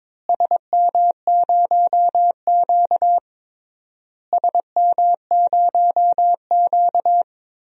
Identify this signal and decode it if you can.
Morse code, heard as a keyed tone, about 22 wpm, 700 Hz. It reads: SM0Q SM0Q
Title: DXCC-100 Call Signs - 4 Characters - Repeated 3x - 22wpm